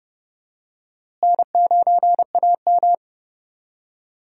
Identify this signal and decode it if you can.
Morse code, heard as a keyed tone, about 31 wpm, 700 Hz. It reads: N9AM
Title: USA Call Signs - 4 Characters - Repeated 2x - 30wpm